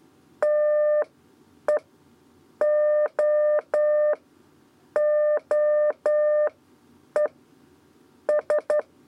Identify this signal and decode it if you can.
Morse code, heard as a keyed tone, about 11 wpm, spaced as for 8 wpm, 600 Hz. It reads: TE O OE S
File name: Lezione trasmessa 2 lez 2 exe